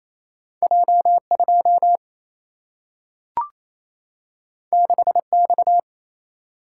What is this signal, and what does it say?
Morse code, heard as a keyed tone, about 28 wpm, 700 Hz. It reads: J 2 6 X
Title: Single-Letter Single-Number 28wpm